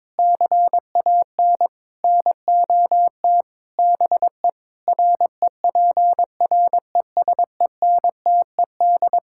Morse code, heard as a keyed tone, about 23 wpm, 700 Hz. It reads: CAN NOT BE REPRESENTED
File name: practice / 22wpm / Build A Phrase - 5-Word Phrases - 22wpm